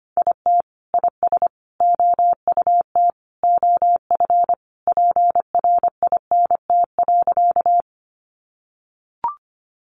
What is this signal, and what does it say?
Morse code, heard as a keyed tone, about 25 wpm, 700 Hz. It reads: IT IS OUT OF PRINT.
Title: Sentences from Top 1400 Words 25wpm